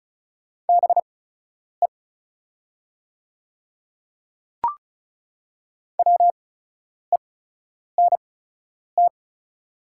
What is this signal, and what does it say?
Morse code, heard as a keyed tone, about 36 wpm, spaced as for 10 wpm, 690 Hz. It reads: BE WENT
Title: 8x Character Spacing - Top 300 Words - 35wpm